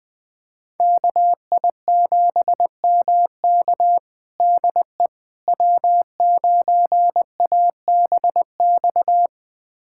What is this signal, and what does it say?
Morse code, heard as a keyed tone, about 20 wpm, 700 Hz. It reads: KI7MK DE W9ABX
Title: USA Callsign DE Callsign - 5 Characters - 20wpm